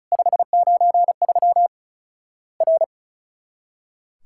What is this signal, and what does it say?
Morse code, heard as a keyed tone, about 36 wpm, 700 Hz. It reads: 593 R
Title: Worked All Europe DX Contest - QTC Exchange - 35wpm